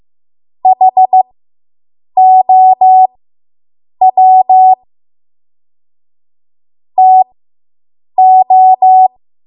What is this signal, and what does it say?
Morse code, heard as a keyed tone, about 15 wpm, spaced as for 7 wpm, 750 Hz. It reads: HOW TO